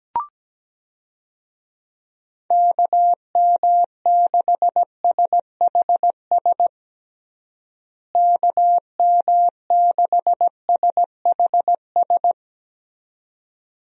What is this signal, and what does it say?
Morse code, heard as a keyed tone, about 17 wpm, 700 Hz. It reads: KM6SHS KM6SHS
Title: USA Call Signs - 6 Characters - Repeated 2x - 17wpm